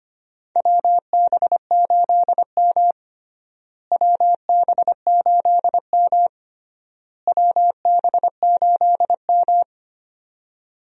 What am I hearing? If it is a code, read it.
Morse code, heard as a keyed tone, about 25 wpm, 700 Hz. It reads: WB8M WB8M WB8M
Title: USA Call Signs - 4 Characters - Repeated 3x - 25wpm